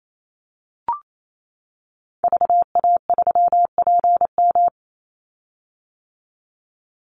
Morse code, heard as a keyed tone, about 28 wpm, 700 Hz. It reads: VA3PM
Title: Call Signs - 28wpm